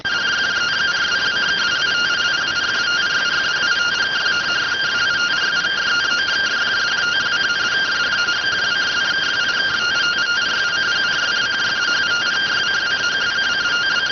Two-tone teleprinter signal, about 170 Hz shift.